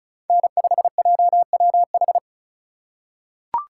Morse code, heard as a keyed tone, about 36 wpm, 700 Hz. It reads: N5JWH
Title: USA Call Signs - 3x Character Spacing - 35wpm